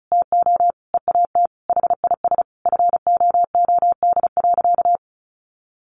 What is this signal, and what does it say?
Morse code, heard as a keyed tone, about 34 wpm, 700 Hz. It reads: TO EAT HIS FOOD.
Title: Sentences from Top 1100 Words 35wpm